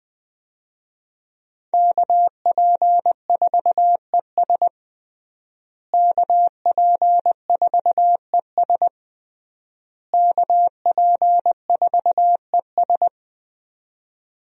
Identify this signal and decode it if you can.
Morse code, heard as a keyed tone, about 20 wpm, 700 Hz. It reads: KP4ES KP4ES KP4ES
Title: USA Call Signs - 5 Characters - Repeated 3x - 20wpm